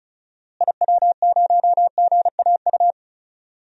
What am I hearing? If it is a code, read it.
Morse code, heard as a keyed tone, about 36 wpm, 700 Hz. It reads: IW0GAU
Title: DXCC-100 Call Signs - 6 Characters - Repeated 3x - 35wpm